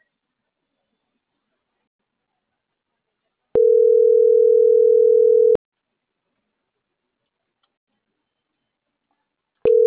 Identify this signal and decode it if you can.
Ringback tone (North America), 440 + 480 Hz.